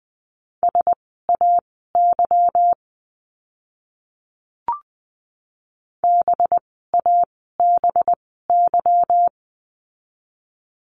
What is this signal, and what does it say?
Morse code, heard as a keyed tone, about 20 wpm, spaced as for 14 wpm, 700 Hz. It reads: SAY BABY
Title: 185 Lesson 30f - Letter b - Words 20wpm